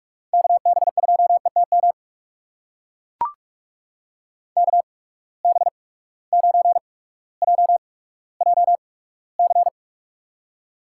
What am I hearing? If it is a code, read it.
Morse code, heard as a keyed tone, about 44 wpm, spaced as for 19 wpm, 700 Hz. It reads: KB2ETM KB9JJC
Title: USA Call Signs - 6 Characters - 8x Character Spacing - 45wpm